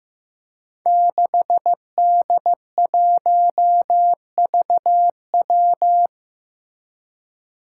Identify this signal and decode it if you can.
Morse code, heard as a keyed tone, about 15 wpm, 700 Hz. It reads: 6D1VW
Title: Five Challenging Letter-Number Combinations 15wpm